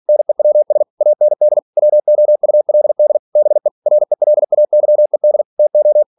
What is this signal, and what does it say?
Morse code, heard as a keyed tone, about 46 wpm, 600 Hz. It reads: NEWS AND WOULD BE RELAYED TO
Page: Never 47wpm